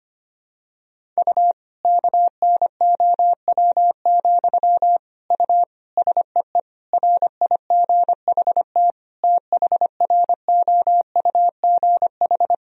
Morse code, heard as a keyed tone, about 25 wpm, 700 Hz. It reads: U KNOW, U SEE RIGHT THROUGH
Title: Sentences from Top 200 Words and Abbreviations 25wpm